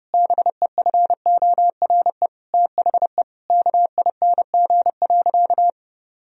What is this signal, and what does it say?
Morse code, heard as a keyed tone, about 32 wpm, 700 Hz. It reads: BEFORE THE KING.
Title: Sentences from Top 1000 Words - Review Part 3 - 30wpm